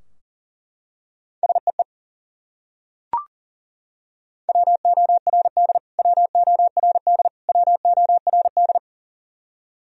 Morse code, heard as a keyed tone, about 40 wpm, 700 Hz. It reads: SEE WORD WORD WORD
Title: Top 100 Words Repeated 40wpm